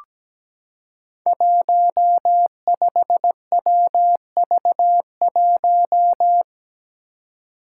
Morse code, heard as a keyed tone, about 17 wpm, 700 Hz. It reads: 15WV1
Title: Five Challenging Letter-Number Combinations 17wpm